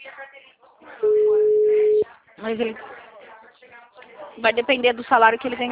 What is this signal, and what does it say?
Ringback tone (Europe, 425 Hz), with speech.